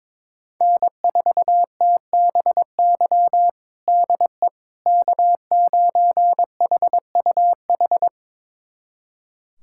Morse code, heard as a keyed tone, about 22 wpm, 700 Hz. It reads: N4TBY DE K9HUH
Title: USA Callsign DE Callsign - 5 Characters - 22wpm